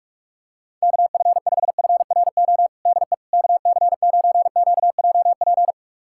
Morse code, heard as a keyed tone, about 46 wpm, 700 Hz. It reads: KF5FRO DE KC9QJP
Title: USA Callsign DE Callsign - 6 Characters - 45wpm